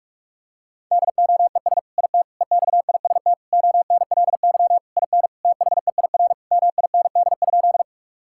Morse code, heard as a keyed tone, about 42 wpm, 700 Hz. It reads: DOES IT EXIST ONLY IN THEIR MIND?
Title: Sentences from Top 1000 Words - Review Part 5 - 45wpm